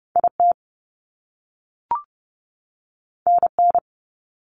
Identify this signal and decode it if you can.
Morse code, heard as a keyed tone, about 30 wpm, 700 Hz. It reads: IT NN
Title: Top Two Letter Combinations in Words 30wpm